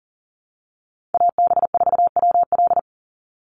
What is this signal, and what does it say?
Morse code, heard as a keyed tone, about 40 wpm, 700 Hz. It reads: AB4WL
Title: NA SOTA Chaser Callsigns 40wpm